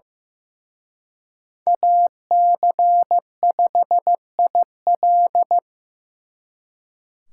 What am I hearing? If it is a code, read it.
Morse code, heard as a keyed tone, about 18 wpm, 700 Hz. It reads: AC5IL
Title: Speed Racing - USA 2x2 Call Signs 15wpm